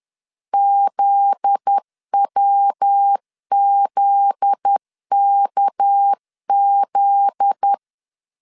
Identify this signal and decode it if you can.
Morse code, heard as a keyed tone, about 10 wpm, 800 Hz. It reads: ZWZKZ
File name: LEZIONI / 7 GF9Z / 7 GF9Z 10wpm 2611